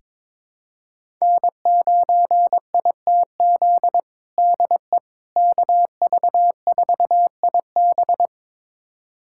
Morse code, heard as a keyed tone, about 22 wpm, 700 Hz. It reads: N9ITZ DE KV4IB
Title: USA Callsign DE Callsign - 5 Characters - 22wpm